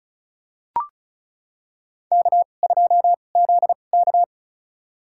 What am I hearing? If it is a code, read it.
Morse code, heard as a keyed tone, about 34 wpm, spaced as for 25 wpm, 700 Hz. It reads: K2ZK
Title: USA Call Signs - 4 Characters - 2x Character Spacing - 35wpm